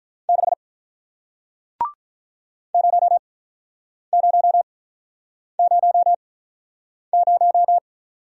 Morse code, heard as a keyed tone, about 48 wpm, spaced as for 9 wpm, 700 Hz. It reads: 6 0000